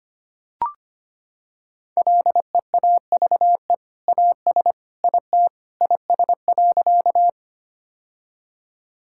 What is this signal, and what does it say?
Morse code, heard as a keyed tone, about 25 wpm, 700 Hz. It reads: LEAVE AS IT IS.
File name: practice / 25wpm / Sentences with 2-Word Phrases - 25wpm